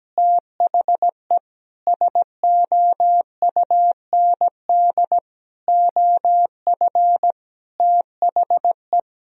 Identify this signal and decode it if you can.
Morse code, heard as a keyed tone, about 17 wpm, 700 Hz. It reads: THE SOUND OF THE
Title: Sentences from Top 1000 Words - Review Part 1 - 17wpm